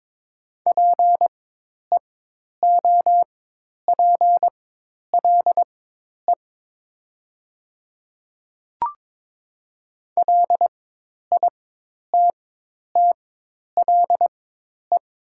Morse code, heard as a keyed tone, about 22 wpm, spaced as for 10 wpm, 690 Hz. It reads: PEOPLE LITTLE
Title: 4x Character Spacing - Top 200 Words - 22wpm